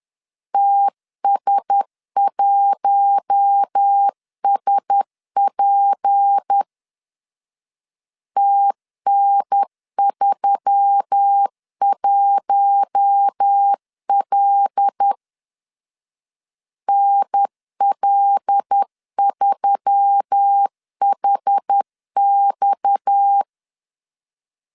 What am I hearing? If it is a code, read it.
Morse code, heard as a keyed tone, about 10 wpm, 800 Hz. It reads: TS1SP TN31L NL3HX
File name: LEZIONI / 6 PKX0 / 6 PKX0 10wpm 2707